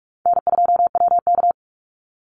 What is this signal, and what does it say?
Morse code, heard as a keyed tone, about 46 wpm, 700 Hz. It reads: N2WK
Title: Easy Call Signs - 45wpm